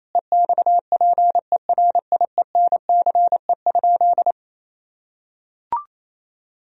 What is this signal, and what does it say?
Morse code, heard as a keyed tone, about 28 wpm, 700 Hz. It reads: EXPERIENCE?
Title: Sentences from Top 1000 Words - Review Part 4 - 28wpm